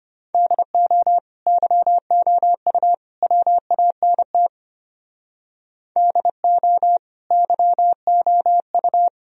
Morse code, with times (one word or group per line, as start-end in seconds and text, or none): 0.3-1.2 DO
1.5-2.9 YOU
3.2-4.5 WANT
6.0-7.0 DO
7.3-9.1 YOU